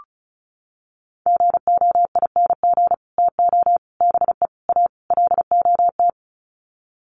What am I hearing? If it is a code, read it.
Morse code, heard as a keyed tone, about 34 wpm, 700 Hz. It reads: GOING TO BE A LOT
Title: Sets of 5 Words Encore 35wpm